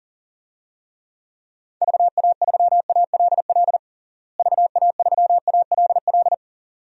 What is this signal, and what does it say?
Morse code, heard as a keyed tone, about 40 wpm, 700 Hz. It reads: VA3ALL VA3ALL
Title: DXCC-100 Call Signs - 6 Characters - Repeated 2x - 40wpm